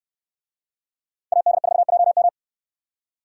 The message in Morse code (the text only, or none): AD4JM